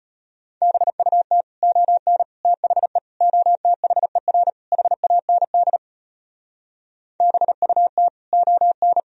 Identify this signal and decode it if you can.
Morse code, heard as a keyed tone, about 33 wpm, 700 Hz. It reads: BUT ON THE OTHER HAND BUT ON